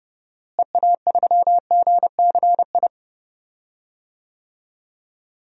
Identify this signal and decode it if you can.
Morse code, heard as a keyed tone, about 29 wpm, 700 Hz. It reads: EA3GCI
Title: DXCC-100 Call Signs - 6 Characters - Repeated 2x - 30wpm